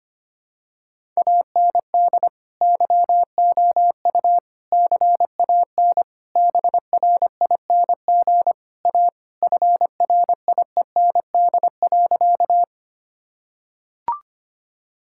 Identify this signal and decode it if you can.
Morse code, heard as a keyed tone, about 25 wpm, 700 Hz. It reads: AND YOU CAN BRING A FRIEND.
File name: practice / 25wpm / Sentences with 2-Word Phrases - 25wpm